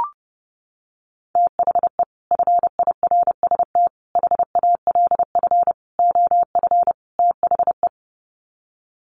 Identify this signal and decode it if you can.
Morse code, heard as a keyed tone, about 30 wpm, 700 Hz. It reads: THE FIRST HALF OF THE
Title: Sets of 5 Words 30wpm